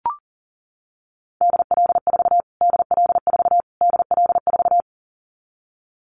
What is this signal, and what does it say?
Morse code, heard as a keyed tone, about 40 wpm, 700 Hz. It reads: DL4 DL4 DL4
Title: Callsign Prefixes for Top 25 DXCC - Repeated 3x - 40wpm